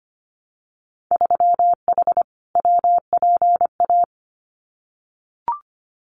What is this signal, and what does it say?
Morse code, heard as a keyed tone, about 25 wpm, 700 Hz. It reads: 3H WPA
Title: Winter Field Day Exchanges Encore 25wpm